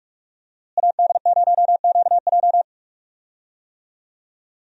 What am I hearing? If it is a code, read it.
Morse code, heard as a keyed tone, about 46 wpm, 700 Hz. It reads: AD0QJ
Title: USA Call Signs - 5 Characters - Repeated 2x - 45wpm